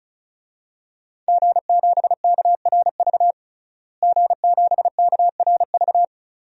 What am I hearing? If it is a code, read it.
Morse code, heard as a keyed tone, about 34 wpm, 700 Hz. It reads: G7KRV G7KRV
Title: DXCC-100 Call Signs - 5 Characters - Repeated 2x - 35wpm